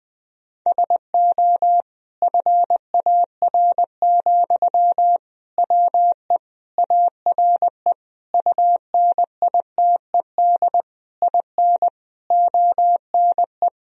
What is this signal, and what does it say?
Morse code, heard as a keyed tone, about 20 wpm, 700 Hz. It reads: SO FAR, WE ARE UNITED IN ONE
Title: Sentences from Top 1000 Words - Review Part 4 - 20wpm